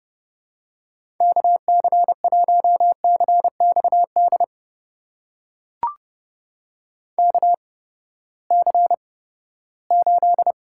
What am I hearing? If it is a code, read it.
Morse code, heard as a keyed tone, about 32 wpm, spaced as for 17 wpm, 700 Hz. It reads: KC1CXD KC8
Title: USA Call Signs - 6 Characters - 8x Character Spacing - 30wpm